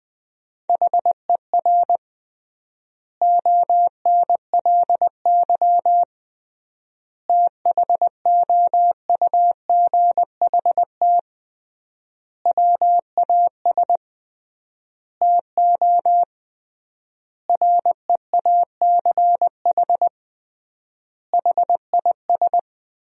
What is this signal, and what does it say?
Morse code, heard as a keyed tone, about 21 wpm, 700 Hz. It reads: HER ONLY THOUGHT WAS TO REACH HIS
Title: 3x Word-Spacing - Sentences from Top 400 Words - 20wpm